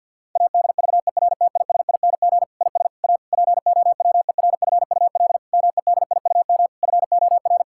Morse code, heard as a keyed tone, about 52 wpm, 700 Hz. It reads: ADVERTISING IS A POWERFUL MEDIUM FOR
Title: Sentences from Top 1000 Words - Review Part 6 - 50wpm